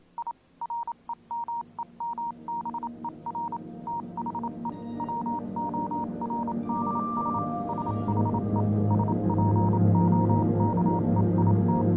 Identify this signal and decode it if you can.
Morse code, heard as a keyed tone, about 27 wpm, 940 Hz, with noise. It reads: I REMEMBER THE WORDS SHE SPOKE IT
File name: Vylet Pony / aura / do you remember the song she sang (morse code version)